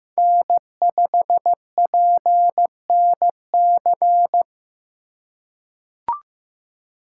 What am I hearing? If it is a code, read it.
Morse code, heard as a keyed tone, about 15 wpm, 700 Hz. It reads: N5PNC
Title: USA Call Signs - 5 Characters - 4x Character Spacing - 15wpm